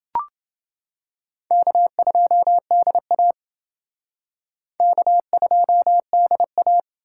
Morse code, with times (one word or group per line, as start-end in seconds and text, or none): 1.5-3.3 K2DA
4.8-6.8 K2DA